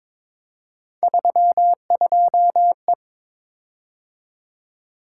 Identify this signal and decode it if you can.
Morse code, heard as a keyed tone, about 22 wpm, 700 Hz. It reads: 32E
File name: practice / 22wpm / Three Letter-Number Combinations - 22wpm